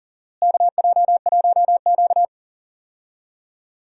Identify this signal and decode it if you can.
Morse code, heard as a keyed tone, about 40 wpm, 700 Hz. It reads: KJ1Q
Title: USA Call Signs - 4 Characters - Repeated 2x - 40wpm